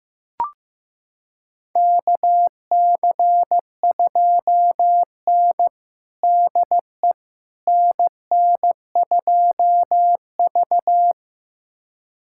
Morse code, heard as a keyed tone, about 15 wpm, 700 Hz. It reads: KC2N DE NN2V
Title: USA Callsign DE Callsign - 4 Characters - 15wpm